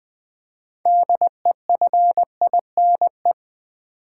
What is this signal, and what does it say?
Morse code, heard as a keyed tone, about 21 wpm, 700 Hz. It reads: DEFINE
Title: 6 Letter Words - First 3 Letters Given - 20wpm